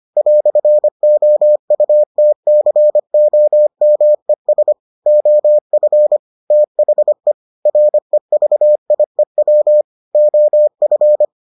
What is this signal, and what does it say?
Morse code, heard as a keyed tone, about 25 wpm, 600 Hz. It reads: "OUTCOMES OF THE REVIEW OF